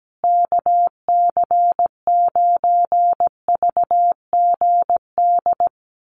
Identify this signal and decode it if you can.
Morse code, heard as a keyed tone, about 17 wpm, 700 Hz. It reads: KC9VGD
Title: USA Call Signs Repeated 3x 17wpm